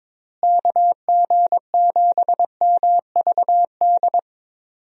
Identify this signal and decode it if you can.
Morse code, heard as a keyed tone, about 21 wpm, 700 Hz. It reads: KG7MVD
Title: USA Call Signs - 6 Characters - Repeated 2x - 22wpm